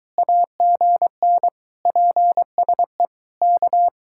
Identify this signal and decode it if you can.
Morse code, heard as a keyed tone, about 22 wpm, 700 Hz. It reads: AGN PSE K